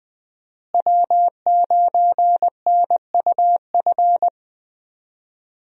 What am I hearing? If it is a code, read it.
Morse code, heard as a keyed tone, about 20 wpm, 700 Hz. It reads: W9NUF
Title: USA Call Signs - 5 Characters - Repeated 3x - 20wpm